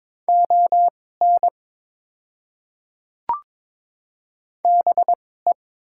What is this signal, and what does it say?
Morse code, heard as a keyed tone, about 22 wpm, spaced as for 16 wpm, 700 Hz. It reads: ON BE
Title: 2x Character Spacing - Top 100 Words - 22wpm